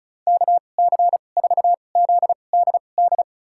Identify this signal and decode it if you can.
Morse code, heard as a keyed tone, about 36 wpm, spaced as for 25 wpm, 700 Hz. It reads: KC4ZDD
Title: USA Call Signs - 6 Characters - 2x Character Spacing - 35wpm